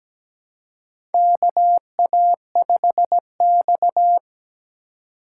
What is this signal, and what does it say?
Morse code, heard as a keyed tone, about 17 wpm, 700 Hz. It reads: KA5X
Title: USA Call Signs - 4 Characters - 2x Character Spacing - 17wpm